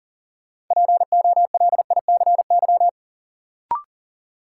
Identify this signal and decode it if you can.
Morse code, heard as a keyed tone, about 40 wpm, 700 Hz. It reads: POLICY
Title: Sentences from Top 1000 Words - Review Part 5 - New Word Drill - 40wpm